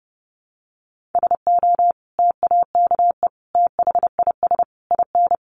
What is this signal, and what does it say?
Morse code, heard as a keyed tone, about 30 wpm, 700 Hz. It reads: SO TAKE THIS IN
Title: Sentences from Top 1500 Words 30wpm